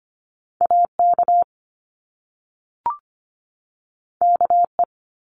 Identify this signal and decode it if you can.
Morse code, heard as a keyed tone, about 25 wpm, 700 Hz. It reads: AK KE
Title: Top Two Letter Combinations in Words 25wpm